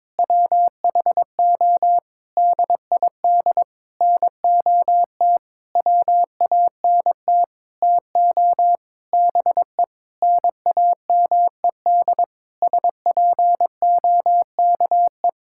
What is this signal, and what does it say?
Morse code, heard as a keyed tone, about 22 wpm, 700 Hz. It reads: WHO DID NOT WANT TO BE NAMED SPOKE
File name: Sentences with 7-Word Phrases - 22wpm